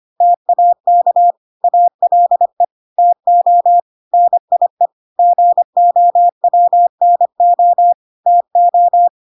Morse code, heard as a keyed tone, about 25 wpm, 700 Hz. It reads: TAK ALE TO NIE GOWNO TO